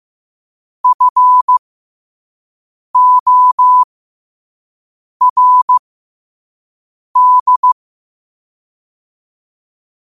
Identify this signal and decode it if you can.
Morse code, heard as a keyed tone, about 15 wpm, spaced as for 5 wpm, 1000 Hz. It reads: FORD
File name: qotdfiles / 5wpm